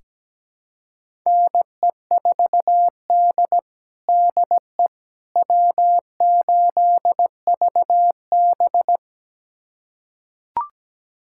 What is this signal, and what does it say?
Morse code, heard as a keyed tone, about 17 wpm, 700 Hz. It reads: NE4D DE W8VB
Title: USA Callsign DE Callsign - 4 Characters - 17wpm